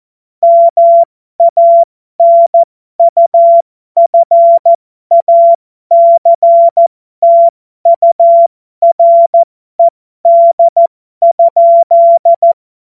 Morse code, heard as a keyed tone, about 14 wpm, spaced as for 12 wpm, 680 Hz. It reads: MANUFACTURED?
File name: Morsekurs / Witze